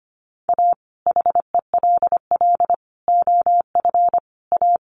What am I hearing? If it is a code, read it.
Morse code, heard as a keyed tone, about 25 wpm, 700 Hz. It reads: A HELL OF A
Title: Sets of 5 Words Encore 25wpm